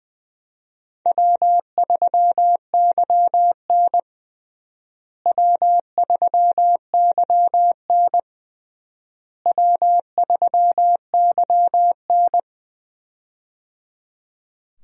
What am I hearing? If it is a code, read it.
Morse code, heard as a keyed tone, about 20 wpm, 700 Hz. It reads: W3YN W3YN W3YN
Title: USA Call Signs - 4 Characters - Repeated 3x - 20wpm